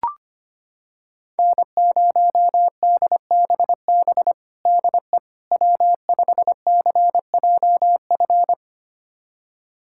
Morse code, heard as a keyed tone, about 25 wpm, 700 Hz. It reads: N0DBB DE W5CJF
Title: USA Callsign DE Callsign - 5 Characters - 25wpm